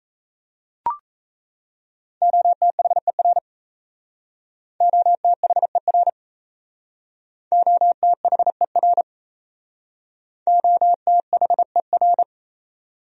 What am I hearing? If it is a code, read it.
Morse code, heard as a keyed tone, about 38 wpm, 700 Hz. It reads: OTHER OTHER OTHER OTHER